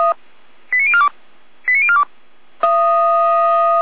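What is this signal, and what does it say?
Five-tone selective call (ZVEI), heard as quick runs of five tones, tones of about 70 ms. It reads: 89042 89042
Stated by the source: ZVEI five-tone selective call received by SDR.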